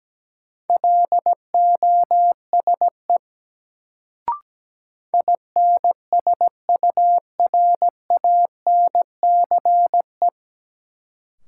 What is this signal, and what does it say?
Morse code, heard as a keyed tone, about 17 wpm, 700 Hz. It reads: LOSE INSURANCE
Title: Sentences from Top 1100 Words - New Word Drill - 17wpm